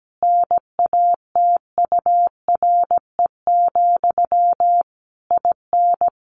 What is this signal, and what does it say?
Morse code, heard as a keyed tone, about 17 wpm, 700 Hz. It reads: NATURE, IN
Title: Sentences from Top 1400 Words 17wpm